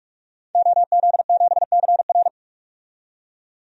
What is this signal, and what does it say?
Morse code, heard as a keyed tone, about 43 wpm, 700 Hz. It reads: OZ7CR
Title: DXCC-100 Call Signs - 5 Characters - 45wpm